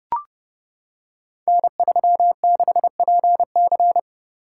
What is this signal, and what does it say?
Morse code, heard as a keyed tone, about 30 wpm, 700 Hz. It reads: N36PC
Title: Code Groups - 5 Letter:Numbers - 30wpm